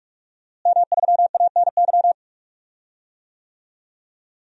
Morse code, heard as a keyed tone, about 42 wpm, 700 Hz. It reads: M3ANY
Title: DXCC-100 Call Signs - 5 Characters - Repeated 3x - 45wpm